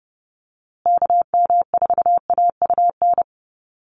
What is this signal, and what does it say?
Morse code, heard as a keyed tone, about 30 wpm, 700 Hz. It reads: KM4AUN
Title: USA Call Signs Repeated 2x 30wpm